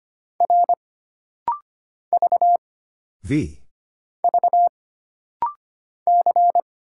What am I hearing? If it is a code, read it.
Morse code, heard as a keyed tone, about 25 wpm, 700 Hz. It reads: R V V C
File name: Single Character (Letter:Number:Punctuation:Prosign) - Rapid-Fire - 25wpm